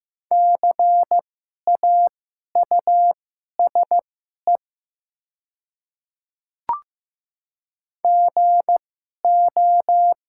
Morse code, heard as a keyed tone, about 15 wpm, spaced as for 11 wpm, 690 Hz. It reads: CAUSE GO
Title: 2x Character Spacing - Top 200 Words - 15wpm